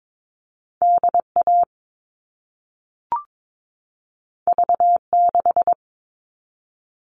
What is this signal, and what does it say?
Morse code, heard as a keyed tone, about 22 wpm, 700 Hz. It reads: DA V6
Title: All Cut Numbers - 2-Digit 22wpm